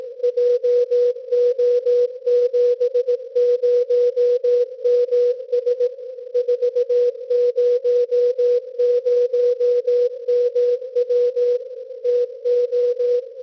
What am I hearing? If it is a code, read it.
Morse code, heard as a keyed tone, about 18 wpm, 490 Hz. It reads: JO70MS 400MW TO
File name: PIXID BEACON / sdr.vebik.cz 1889.51 cwn